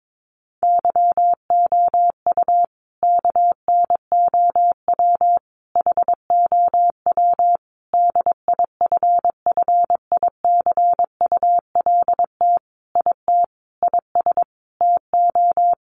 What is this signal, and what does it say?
Morse code, heard as a keyed tone, about 22 wpm, 700 Hz. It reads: YOU KNOW HOW DIFFICULT IT IS TO
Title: Sentences from Top 1100 Words 22wpm